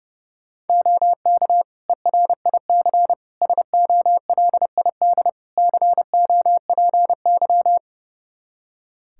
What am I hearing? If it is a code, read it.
Morse code, heard as a keyed tone, about 30 wpm, 700 Hz. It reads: OK ERIC SOLID COPY
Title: Practice QSO - Exchange 3 - More Info - 30wpm